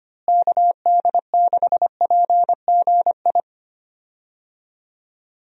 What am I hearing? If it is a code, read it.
Morse code, heard as a keyed tone, about 25 wpm, 700 Hz. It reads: KD6PGI
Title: USA Call Signs - 6 Characters - Repeated 3x - 25wpm